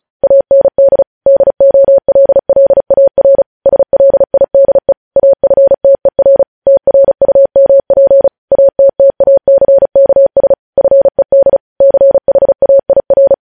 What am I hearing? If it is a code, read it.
Morse code, heard as a keyed tone, about 36 wpm, 550 Hz. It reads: AND DOLLAR SLIDE AFTER TRUMP ATTACKS FED CHAIR
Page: news headlines in morse code at 35 wpm